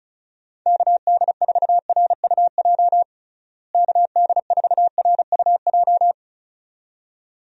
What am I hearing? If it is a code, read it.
Morse code, heard as a keyed tone, about 34 wpm, 700 Hz. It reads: KD4RUJ KD4RUJ
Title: USA Call Signs - 6 Characters - Repeated 2x - 35wpm